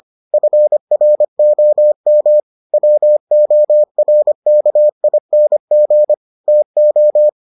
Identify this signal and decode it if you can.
Morse code, heard as a keyed tone, about 25 wpm, 600 Hz. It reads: FROM WORKING TO